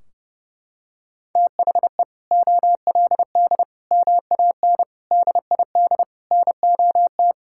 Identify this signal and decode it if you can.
Morse code, heard as a keyed tone, about 30 wpm, 700 Hz. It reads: THE OLD MAN DID NOT
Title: Sentences from Top 1600 Words 30wpm